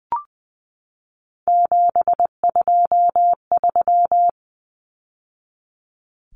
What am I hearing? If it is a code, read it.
Morse code, heard as a keyed tone, about 20 wpm, 700 Hz. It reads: 723
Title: Numbers - 3-Digit 20wpm